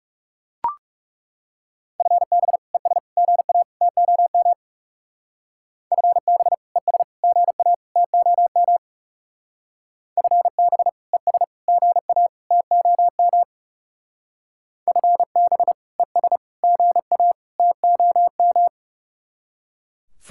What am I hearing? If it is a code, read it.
Morse code, heard as a keyed tone, about 36 wpm, 700 Hz. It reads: FB ES GA TOM FB ES GA TOM FB ES GA TOM FB ES GA TOM
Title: Speed-Racing Common QSO Phrases - 30wpm